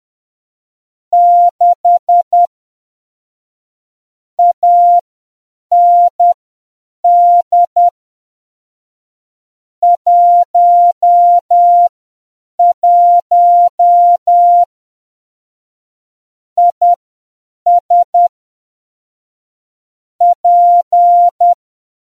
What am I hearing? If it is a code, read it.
Morse code, heard as a keyed tone, about 10 wpm, spaced as for 7 wpm, 700 Hz. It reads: 6 AND 11 IS P